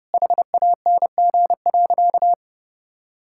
Morse code, heard as a keyed tone, about 32 wpm, 700 Hz. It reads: HANG.
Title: Sentences from Top 1600 Words 30wpm